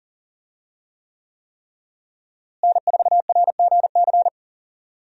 Morse code, heard as a keyed tone, about 39 wpm, 700 Hz. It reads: N4RGC
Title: USA Call Signs - 5 Characters - Repeated 3x - 40wpm